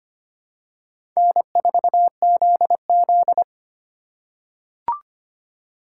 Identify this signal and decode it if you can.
Morse code, heard as a keyed tone, about 25 wpm, 700 Hz. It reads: N4ZZ
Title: USA Call Signs - 4 Characters - 6x Character Spacing - 25wpm